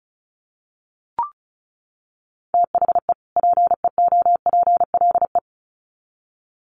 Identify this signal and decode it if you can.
Morse code, heard as a keyed tone, about 34 wpm, 700 Hz. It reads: THE PEOPLE
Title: Sets of 2 Words 35wpm